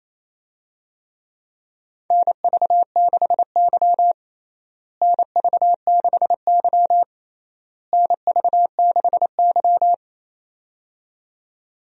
Morse code, heard as a keyed tone, about 28 wpm, 700 Hz. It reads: NV6Y NV6Y NV6Y
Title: USA Call Signs - 4 Characters - Repeated 3x - 28wpm